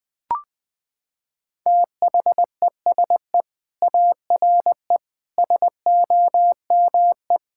Morse code, heard as a keyed tone, about 20 wpm, 700 Hz. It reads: THESE ARE SOME
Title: Sentences with 4-Word Phrases - 20wpm